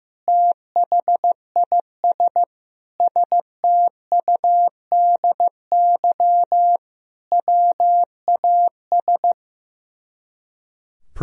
Morse code, heard as a keyed tone, about 15 wpm, 700 Hz. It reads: THIS STUDY WAS
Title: Speed-Racing - Sets of 5 Words 15wpm